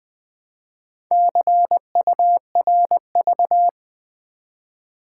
Morse code, heard as a keyed tone, about 20 wpm, 700 Hz. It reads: CURV
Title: Build A Word - 1 Letter At A Time - 3 to 5-Letter Words - 20wpm